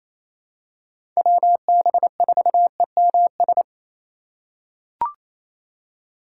Morse code, heard as a keyed tone, about 28 wpm, 700 Hz. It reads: WB4EMS
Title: USA Call Signs - 2x Character Spacing - 28wpm